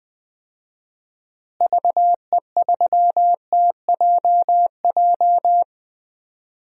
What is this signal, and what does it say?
Morse code, heard as a keyed tone, about 20 wpm, 700 Hz. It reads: VE3TJJ
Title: DXCC-100 Call Signs - 6 Characters - Repeated 3x - 20wpm